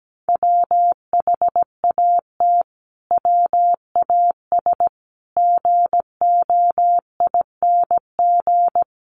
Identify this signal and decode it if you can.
Morse code, heard as a keyed tone, about 17 wpm, 700 Hz. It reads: WHAT WAS GOING
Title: Sets of 5 Words Encore 17wpm